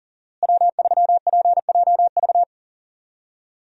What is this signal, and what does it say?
Morse code, heard as a keyed tone, about 40 wpm, 700 Hz. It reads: W3PJV
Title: USA Call Signs - 5 Characters - Repeated 3x - 40wpm